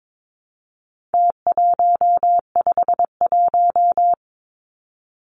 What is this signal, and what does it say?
Morse code, heard as a keyed tone, about 22 wpm, 700 Hz. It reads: T151
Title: Numbers - Contest Style - 4-Digit 22wpm